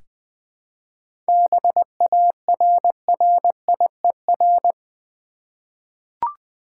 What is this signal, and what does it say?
Morse code, heard as a keyed tone, about 20 wpm, 700 Hz. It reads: BARRIER
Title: 7 Letter Words - First 3 Letters Given - 20wpm